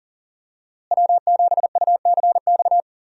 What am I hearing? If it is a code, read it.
Morse code, heard as a keyed tone, about 40 wpm, 700 Hz. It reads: W7UCX
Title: USA Call Signs - 5 Characters - Repeated 3x - 40wpm